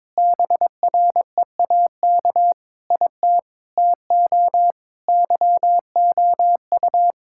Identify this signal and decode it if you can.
Morse code, heard as a keyed tone, about 22 wpm, 700 Hz. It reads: BREAK IT TO YOU
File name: practice / 22wpm / Sets of 8 Words - 22wpm